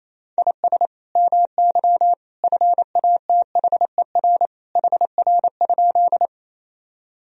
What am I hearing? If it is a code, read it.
Morse code, heard as a keyed tone, about 28 wpm, 700 Hz. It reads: IS MY FATHER HR?
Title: Sentences from Top 300 Words and Abbreviations 28wpm